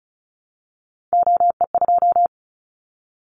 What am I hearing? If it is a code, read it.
Morse code, heard as a keyed tone, about 36 wpm, 700 Hz. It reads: OE2
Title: Callsign Prefixes for Top 25 DXCC - 35wpm